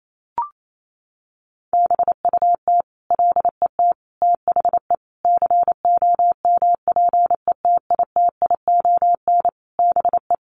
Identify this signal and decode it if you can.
Morse code, heard as a keyed tone, about 28 wpm, 700 Hz. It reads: BUT LET THE COMPETITION BE
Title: Sentences from Top 1300 Words 28wpm